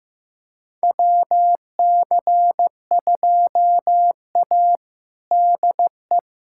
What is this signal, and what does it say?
Morse code, heard as a keyed tone, about 16 wpm, 700 Hz. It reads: WC2A DE
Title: USA Callsign DE Callsign - 4 Characters - 15wpm